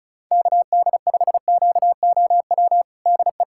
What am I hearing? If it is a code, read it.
Morse code, heard as a keyed tone, about 36 wpm, 700 Hz. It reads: KD5QOW DE
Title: USA Callsign DE Callsign 35wpm